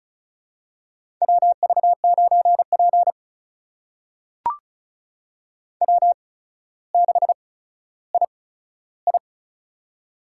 Morse code, heard as a keyed tone, about 34 wpm, spaced as for 15 wpm, 700 Hz. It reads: WV9P W6II
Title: USA Call Signs - 4 Characters - 8x Character Spacing - 35wpm